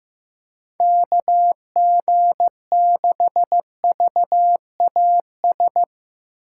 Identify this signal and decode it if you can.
Morse code, heard as a keyed tone, about 16 wpm, 700 Hz. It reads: KG6VAS
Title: Speed Racing - USA Call Signs 15wpm